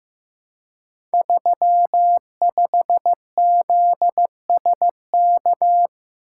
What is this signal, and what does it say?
Morse code, heard as a keyed tone, about 15 wpm, 700 Hz. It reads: 35ZSK
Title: Code Groups - 5 Letter:Numbers - 15wpm